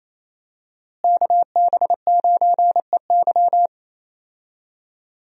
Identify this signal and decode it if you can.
Morse code, heard as a keyed tone, about 28 wpm, 700 Hz. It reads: KB9EY
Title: USA Call Signs - 5 Characters - Repeated 2x - 28wpm